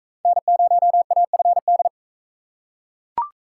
Morse code, heard as a keyed tone, about 42 wpm, 700 Hz. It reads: N0AFD